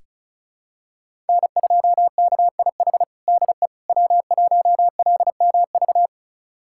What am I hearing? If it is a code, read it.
Morse code, heard as a keyed tone, about 36 wpm, 700 Hz. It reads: N2KIH DE W1LMV
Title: USA Callsign DE Callsign - 5 Characters - 35wpm